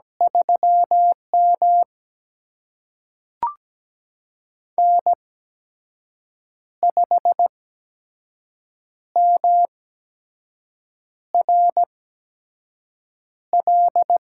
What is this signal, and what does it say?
Morse code, heard as a keyed tone, about 17 wpm, spaced as for 5 wpm, 700 Hz. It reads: 3M N5MRL
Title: USA Call Signs - 8x Character Spacing - 17wpm